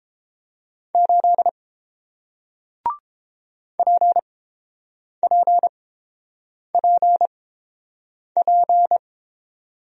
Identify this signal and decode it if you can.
Morse code, heard as a keyed tone, about 31 wpm, spaced as for 7 wpm, 700 Hz. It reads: 8 PPPP